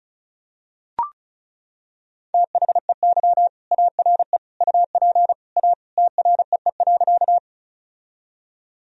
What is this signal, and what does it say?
Morse code, heard as a keyed tone, about 36 wpm, 700 Hz. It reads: THEY ARE UP A TREE.
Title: Sentences from Top 1000 Words - Review Part 1 - 35wpm